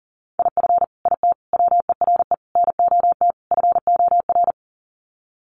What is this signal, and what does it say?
Morse code, heard as a keyed tone, about 40 wpm, 700 Hz. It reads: IF IT WERE NOT FOR
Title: Sets of 5 Words 40wpm